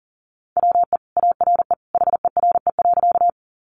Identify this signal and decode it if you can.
Morse code, heard as a keyed tone, about 37 wpm, 700 Hz. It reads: WE ARE HERE.
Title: Sentences from Top 200 Words 40wpm